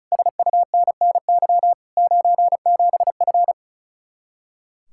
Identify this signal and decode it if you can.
Morse code, heard as a keyed tone, about 47 wpm, spaced as for 41 wpm, 700 Hz. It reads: SUNNY 97F
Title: Speed-Racing Common QSO Phrases Encore - 35wpm